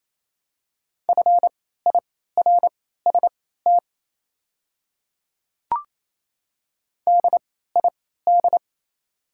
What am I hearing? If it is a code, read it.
Morse code, heard as a keyed tone, about 27 wpm, spaced as for 16 wpm, 690 Hz. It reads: FIRST DID
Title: 3x Character Spacing - Top 100 Words - 28wpm